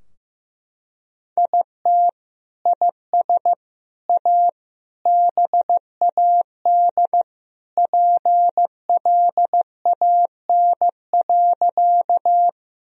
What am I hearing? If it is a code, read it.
Morse code, heard as a keyed tone, about 15 wpm, 700 Hz. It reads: IT IS A BAD PLAN.
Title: Sentences from Top 1000 Words - Review Part 2 - 15wpm